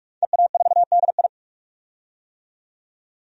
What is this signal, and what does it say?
Morse code, heard as a keyed tone, about 42 wpm, 700 Hz. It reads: EA4DI
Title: DXCC-100 Call Signs - 5 Characters - 45wpm